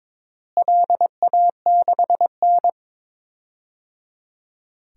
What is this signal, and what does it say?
Morse code, heard as a keyed tone, about 22 wpm, 700 Hz. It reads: LA6N
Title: DXCC-100 Call Signs - 4 Characters - Repeated 3x - 22wpm